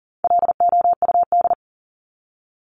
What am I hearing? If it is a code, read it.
Morse code, heard as a keyed tone, about 40 wpm, 700 Hz. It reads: LOUD
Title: Top 1751-2000 Words 40wpm